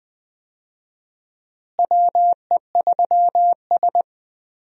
Morse code, heard as a keyed tone, about 20 wpm, 700 Hz. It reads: WE3S
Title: USA Call Signs - 4 Characters - Repeated 2x - 20wpm